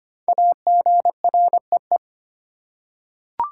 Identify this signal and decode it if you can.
Morse code, heard as a keyed tone, about 25 wpm, 700 Hz. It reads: AGREE
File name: 5 Letter Words - First 3 Letters Given - 25wpm